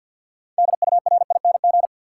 Morse code, heard as a keyed tone, about 52 wpm, 700 Hz. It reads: DURING
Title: Six Letter Words - 50wpm